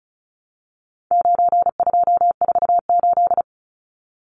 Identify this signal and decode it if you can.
Morse code, heard as a keyed tone, about 34 wpm, 700 Hz. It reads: 9248
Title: Numbers - 4-Digit 35wpm